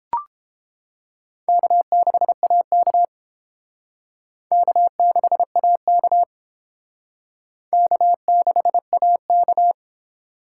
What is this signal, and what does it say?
Morse code, heard as a keyed tone, about 29 wpm, 700 Hz. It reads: K6AK K6AK K6AK